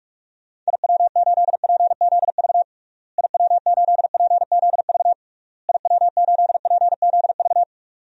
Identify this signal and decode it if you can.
Morse code, heard as a keyed tone, about 41 wpm, 700 Hz. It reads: IW8PZV IW8PZV IW8PZV
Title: DXCC-100 Call Signs - 6 Characters - Repeated 3x - 45wpm